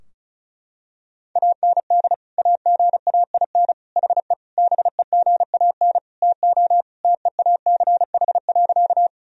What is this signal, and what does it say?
Morse code, heard as a keyed tone, about 36 wpm, 700 Hz. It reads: AND AGAIN HE BEGAN TO TEACH.
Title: Sentences from Top 1000 Words - Review Part 2 - 35wpm